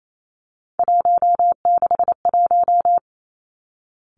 Morse code, heard as a keyed tone, about 28 wpm, 700 Hz. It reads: 161
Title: Three Challenging Letter-Number Combinations 28wpm